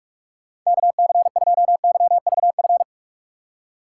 Morse code, heard as a keyed tone, about 44 wpm, 700 Hz. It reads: KC2YVF
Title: USA Call Signs - 6x Character Spacing - 45wpm